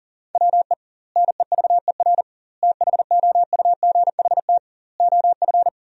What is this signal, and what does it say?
Morse code, heard as a keyed tone, about 40 wpm, 700 Hz. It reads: WE NEVER THOUGHT OF
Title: 2x Word-Spacing - Sentences from Top 300 Words - 40wpm